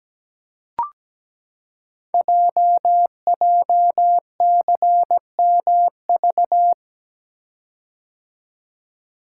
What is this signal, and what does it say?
Morse code, heard as a keyed tone, about 17 wpm, 700 Hz. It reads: JJCMV
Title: Code Groups - 5 Letters - 17wpm